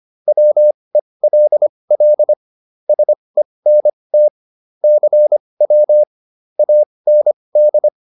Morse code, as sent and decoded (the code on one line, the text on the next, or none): .-- . .-.. .-.. / ... -. - / -.-. .-- / .- -. -..
WELL SENT CW AND